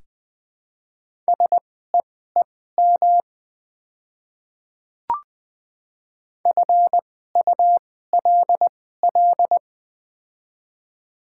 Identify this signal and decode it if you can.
Morse code, heard as a keyed tone, about 20 wpm, spaced as for 15 wpm, 700 Hz. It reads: SEEM FULL
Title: 2x Character Spacing - Top 500 Words - 20wpm